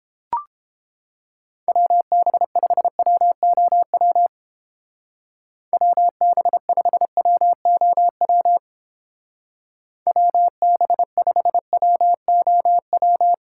Morse code, heard as a keyed tone, about 31 wpm, 700 Hz. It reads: WB5WOW WB5WOW WB5WOW